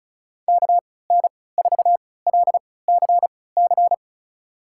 Morse code, heard as a keyed tone, about 36 wpm, spaced as for 20 wpm, 700 Hz. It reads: KN4LCC
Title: USA Call Signs - 6 Characters - 3x Character Spacing - 35wpm